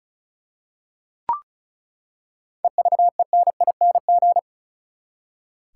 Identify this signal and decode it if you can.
Morse code, heard as a keyed tone, about 36 wpm, 700 Hz. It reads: EVENING
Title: Seven Letter Words - 35wpm